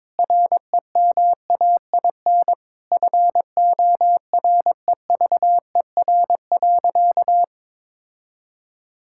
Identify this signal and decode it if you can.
Morse code, heard as a keyed tone, about 22 wpm, 700 Hz. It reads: REMAIN FOREVER.
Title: Sentences from Top 1600 Words 22wpm